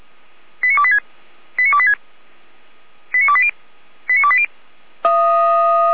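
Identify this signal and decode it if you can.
Five-tone selective call (ZVEI), heard as quick runs of five tones, tones of about 70 ms. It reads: 89286 89286 89280 89280
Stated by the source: Five-tone ZVEI selective call received on an SDR.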